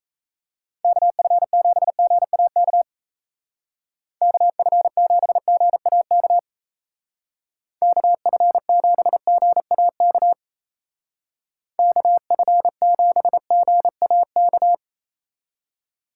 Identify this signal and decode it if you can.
Morse code, heard as a keyed tone, about 34 wpm, 700 Hz. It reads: KF7GAK KF7GAK KF7GAK KF7GAK
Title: Speed Racing - USA 2x3 Call Signs 28wpm